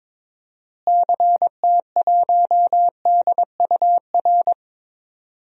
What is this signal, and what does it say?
Morse code, heard as a keyed tone, about 22 wpm, 700 Hz. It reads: CT1DUR
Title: DXCC-100 Call Signs - 6 Characters - Repeated 3x - 22wpm